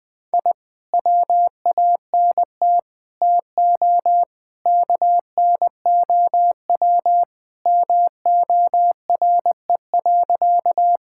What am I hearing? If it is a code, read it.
Morse code, heard as a keyed tone, about 20 wpm, 700 Hz. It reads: I WANT TO KNOW MORE.
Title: Sentences with 2-Word Phrases - Spoken Beginning - 20wpm